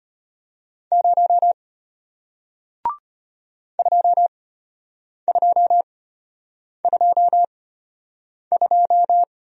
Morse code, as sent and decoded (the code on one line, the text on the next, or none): ----- / ..--- ..--- ..--- ..---
0 2222